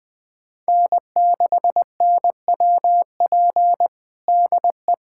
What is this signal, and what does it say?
Morse code, heard as a keyed tone, about 20 wpm, 700 Hz. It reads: N6NWP DE
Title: USA Callsign DE Callsign - 5 Characters - 20wpm